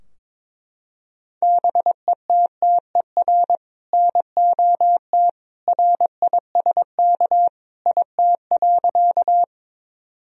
Morse code, heard as a keyed tone, about 22 wpm, 700 Hz. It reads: BETTER NOT RISK IT.
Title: Sentences from Top 1000 Words - Review Part 4 - 22wpm